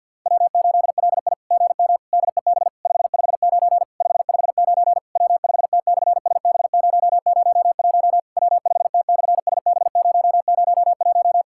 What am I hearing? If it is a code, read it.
Morse code, heard as a keyed tone, about 51 wpm, 690 Hz. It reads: W8LI GM DEB 559 559 W5T/SB001 W5T/SB001
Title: SOTA QSO Practice 50wpm